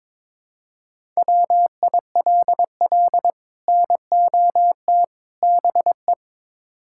Morse code, heard as a keyed tone, about 22 wpm, 700 Hz. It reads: WILL NOT BE
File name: practice / 22wpm / Build A Phrase - 5-Word Phrases - 22wpm